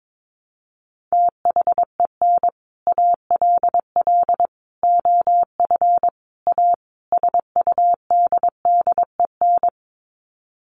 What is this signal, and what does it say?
Morse code, heard as a keyed tone, about 22 wpm, 700 Hz. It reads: THEN ALL OF A SUDDEN
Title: Sets of 5 Words Encore 22wpm